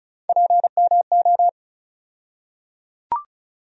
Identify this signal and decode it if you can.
Morse code, heard as a keyed tone, about 36 wpm, 700 Hz. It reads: PMO
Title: USA Call Sign Postfixes - 3 Letters 35wpm